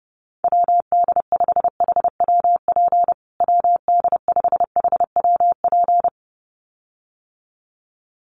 Morse code, heard as a keyed tone, about 30 wpm, 700 Hz. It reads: WD5HWP WD5HWP
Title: USA Call Signs Repeated 2x 30wpm